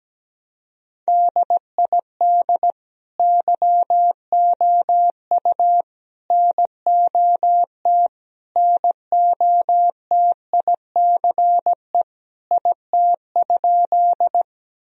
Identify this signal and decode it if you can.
Morse code, heard as a keyed tone, about 17 wpm, 700 Hz. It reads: DID YOU NOT NOTICE IT?
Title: Sentences from Top 1000 Words - Review Part 2 - 17wpm